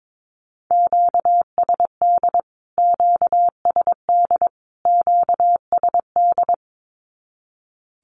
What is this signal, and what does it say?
Morse code, heard as a keyed tone, about 22 wpm, 700 Hz. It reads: QSD QSD QSD
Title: Q-Codes Repeated 3x 22wpm